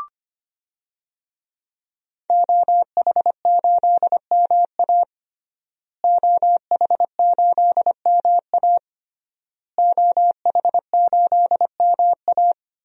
Morse code, heard as a keyed tone, about 25 wpm, 700 Hz. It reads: OH8MA OH8MA OH8MA
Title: DXCC-100 Call Signs - 5 Characters - Repeated 3x - 25wpm